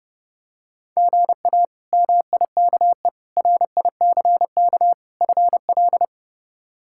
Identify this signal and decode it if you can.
Morse code, heard as a keyed tone, about 32 wpm, 700 Hz. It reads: GA MIKE RICK FL
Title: K1USN SST QSO Practice - 30wpm